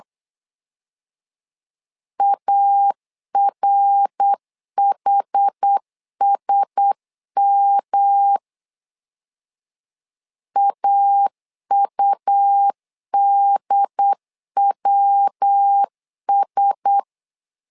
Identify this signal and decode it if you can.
Morse code, heard as a keyed tone, about 8 wpm, 800 Hz. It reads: ARHSM AUDWS
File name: LEZIONI / 4 Q1S 8wpm 2710